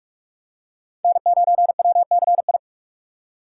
Morse code, heard as a keyed tone, about 43 wpm, 700 Hz. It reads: N9WCI
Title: USA Call Signs - 5 Characters - 2x Character Spacing - 45wpm